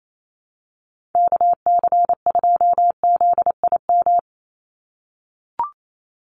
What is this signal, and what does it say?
Morse code, heard as a keyed tone, about 28 wpm, 700 Hz. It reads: KC2ZIM
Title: USA Call Signs 28wpm